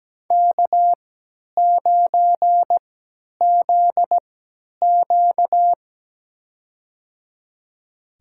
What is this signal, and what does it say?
Morse code, heard as a keyed tone, about 17 wpm, spaced as for 10 wpm, 700 Hz. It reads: K9ZQ
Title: USA Call Signs - 4 Characters - 3x Character Spacing - 17wpm